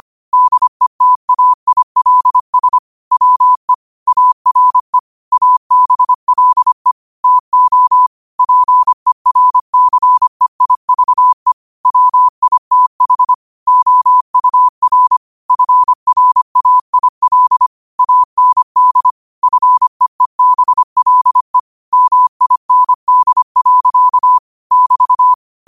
Morse code, heard as a keyed tone, about 25 wpm, 1000 Hz. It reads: DETAILS WE ARE ABLE TO PERCEIVE WITH OUR FRAIL AND FEEBLE MIND. <BT>
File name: qotdfiles / 25wpm